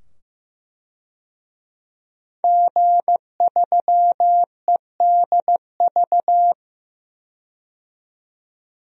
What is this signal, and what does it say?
Morse code, heard as a keyed tone, about 15 wpm, 700 Hz. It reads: G3EDV
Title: DXCC-100 Call Signs - 5 Characters - 15wpm